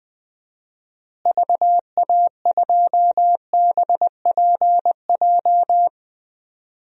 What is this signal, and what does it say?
Morse code, heard as a keyed tone, about 20 wpm, 700 Hz. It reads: VA2BPJ
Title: DXCC-100 Call Signs - 6 Characters - Repeated 3x - 20wpm